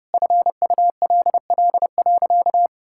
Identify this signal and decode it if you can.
Morse code, heard as a keyed tone, about 30 wpm, 700 Hz. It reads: FULL.
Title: Sentences from Top 1600 Words 30wpm